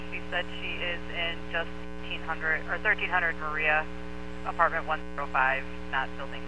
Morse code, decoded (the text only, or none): none